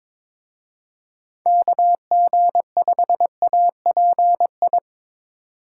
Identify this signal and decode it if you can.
Morse code, heard as a keyed tone, about 26 wpm, 700 Hz. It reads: KG5API
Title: Speed Racing - USA 2x3 Call Signs 22wpm